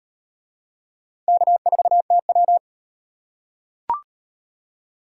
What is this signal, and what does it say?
Morse code, heard as a keyed tone, about 38 wpm, 700 Hz. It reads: K4TW